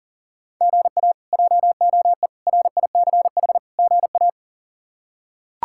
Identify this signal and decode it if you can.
Morse code, heard as a keyed tone, about 40 wpm, 700 Hz. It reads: GA JOE RICH GA
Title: K1USN SST QSO Practice - 40wpm